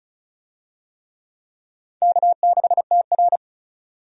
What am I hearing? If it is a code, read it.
Morse code, heard as a keyed tone, about 34 wpm, 700 Hz. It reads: K6TR
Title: USA Call Signs - 4 Characters - 35wpm